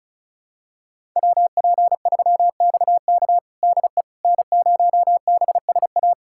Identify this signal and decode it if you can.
Morse code, heard as a keyed tone, about 36 wpm, 700 Hz. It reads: WP3XK DE N0BSA
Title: USA Callsign DE Callsign - 5 Characters - 35wpm